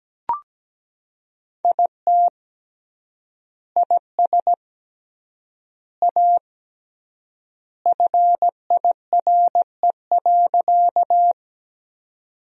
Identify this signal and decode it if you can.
Morse code, heard as a keyed tone, about 17 wpm, 700 Hz. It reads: IT IS A FIRE.
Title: 3x Word-Spacing - Sentences from Top 400 Words - 17wpm